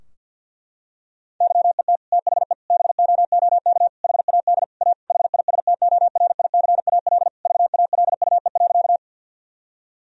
Morse code, heard as a keyed tone, about 50 wpm, 700 Hz. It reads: YET THE BOOK HAD A HISTORICAL VALUE.
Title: Sentences from Top 1000 Words - Review Part 4 - 50wpm